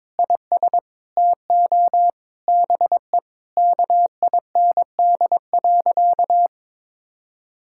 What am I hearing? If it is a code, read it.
Morse code, heard as a keyed tone, about 22 wpm, 700 Hz. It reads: IS TO BE KIND.
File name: practice / 22wpm / Sentences with 4-Word Phrases - 22wpm